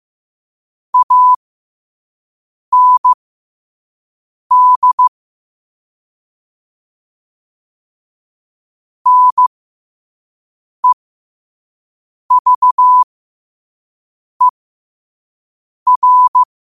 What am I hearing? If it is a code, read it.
Morse code, heard as a keyed tone, about 15 wpm, spaced as for 5 wpm, 1000 Hz. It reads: AND NEVER
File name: qotdfiles / 5wpm